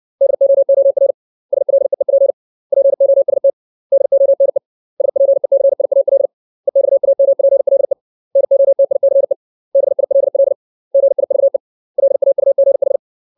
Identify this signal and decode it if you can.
Morse code, heard as a keyed tone, about 60 wpm, 550 Hz. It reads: DOOR SLEEP POST DONE SPECIAL EXAMPLE NOTICE BILL GIVE LAUGH
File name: 632 words / 632 Words 60WPM